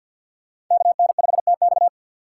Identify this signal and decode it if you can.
Morse code, heard as a keyed tone, about 49 wpm, 690 Hz. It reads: KN5TX
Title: K1USN SST QSO Practice - 50wpm